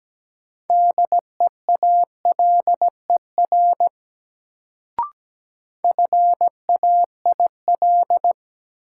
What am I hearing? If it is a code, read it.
Morse code, heard as a keyed tone, about 17 wpm, 700 Hz. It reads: DEALER FAIL
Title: Sentences from Top 1100 Words - New Word Drill Alt - 17wpm